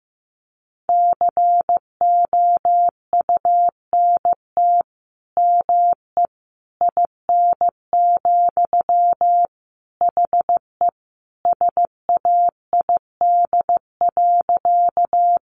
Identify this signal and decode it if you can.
Morse code, heard as a keyed tone, about 14 wpm, 700 Hz. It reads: COUNT ME IN, HE SAID.
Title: Sentences from Top 1500 Words 15wpm